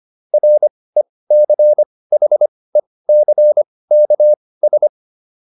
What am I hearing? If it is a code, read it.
Morse code, heard as a keyed tone, about 25 wpm, spaced as for 18 wpm, 600 Hz. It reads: RECHECKS